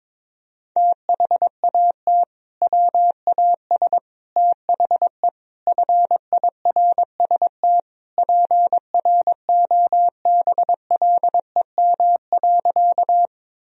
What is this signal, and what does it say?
Morse code, heard as a keyed tone, about 21 wpm, 700 Hz. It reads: THAT WAS THE FIRST PROBLEM.
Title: Sentences from Top 1000 Words - Review Part 2 - 22wpm